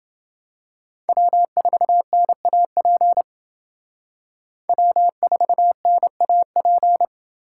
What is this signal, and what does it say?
Morse code, heard as a keyed tone, about 29 wpm, 700 Hz. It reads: W4NAP W4NAP